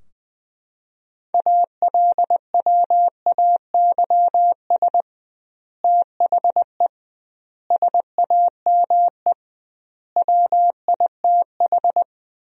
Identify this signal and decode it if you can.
Morse code, heard as a keyed tone, about 20 wpm, 700 Hz. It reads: ALWAYS THE SAME WITH
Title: 2x Word-Spacing - Sentences from Top 300 Words - 20wpm